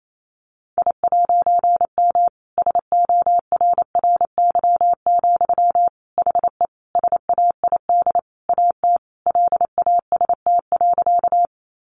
Morse code, heard as a keyed tone, about 28 wpm, 700 Hz. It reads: I'M SORRY, HE SAID AT LAST.
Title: A Sound Of Thunder 28wpm